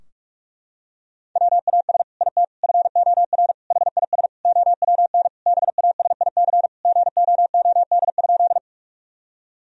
Morse code, heard as a keyed tone, about 42 wpm, 700 Hz. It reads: WAS IT FOR HIS OWN BASIC GOOD?
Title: Sentences from Top 1000 Words - Review Part 3 - 45wpm